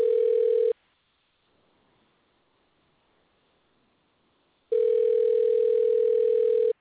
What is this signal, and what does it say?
Ringback tone (North America), 440 + 480 Hz.